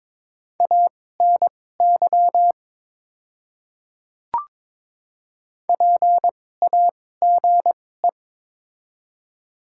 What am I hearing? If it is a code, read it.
Morse code, heard as a keyed tone, about 22 wpm, spaced as for 16 wpm, 700 Hz. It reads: ANY PAGE
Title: 2x Character Spacing - Top 300 Words - 22wpm